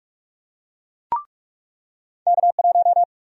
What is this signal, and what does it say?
Morse code, heard as a keyed tone, about 46 wpm, 700 Hz. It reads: K1
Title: USA Call Sign Prefixes - 45wpm